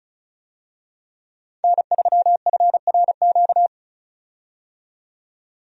Morse code, heard as a keyed tone, about 36 wpm, 700 Hz. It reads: N3FRQ
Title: USA Call Signs - 5 Characters - 35wpm